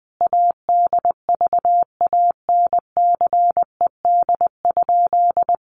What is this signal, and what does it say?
Morse code, heard as a keyed tone, about 19 wpm, 700 Hz. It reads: ADVANCED?
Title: Sentences from Top 1200 Words 20wpm